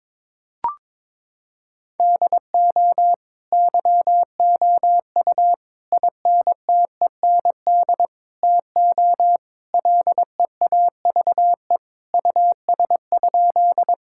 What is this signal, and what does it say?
Morse code, heard as a keyed tone, about 21 wpm, 700 Hz. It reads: DO YOU INTEND TO LEAVE US?
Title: Sentences from Top 1600 Words 22wpm